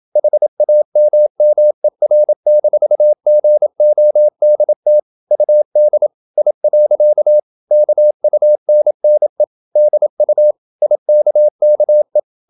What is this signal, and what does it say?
Morse code, heard as a keyed tone, about 27 wpm, 600 Hz. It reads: HAMMER-GODT UD I. KUNNE DU IKKE
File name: Ferie QRQ 27wpm